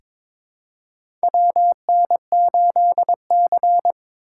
Morse code, heard as a keyed tone, about 22 wpm, 700 Hz. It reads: WN8C
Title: USA Call Signs - 6x Character Spacing - 22wpm